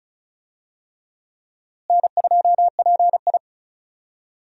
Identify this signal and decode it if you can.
Morse code, heard as a keyed tone, about 36 wpm, 700 Hz. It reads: N2PI
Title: USA Call Signs - 4 Characters - Repeated 3x - 35wpm